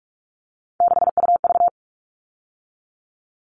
Morse code, heard as a keyed tone, about 44 wpm, 700 Hz. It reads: 6UV
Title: All Cut Numbers - 3-Digit 45wpm